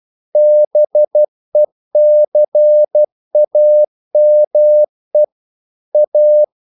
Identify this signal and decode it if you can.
Morse code, heard as a keyed tone, about 12 wpm, 600 Hz. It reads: BECAME A
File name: CW / Practice / EnglishQuotations0112